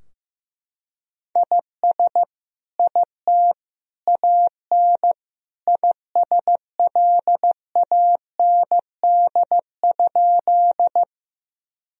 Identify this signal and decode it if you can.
Morse code, heard as a keyed tone, about 15 wpm, 700 Hz. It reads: IS IT AN ISLAND?
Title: Sentences from Top 1000 Words - Review Part 3 - 15wpm